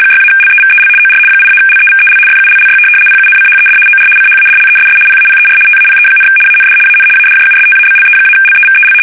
RTTY, about 75 baud. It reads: 6-LEVEL OR 8-LEVEL FULLY PUNCHED OR CHADLESS PAPER TAPE, AND AN ADJUSTABLE-LEVEL
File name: fsk850-75B / teletype-corp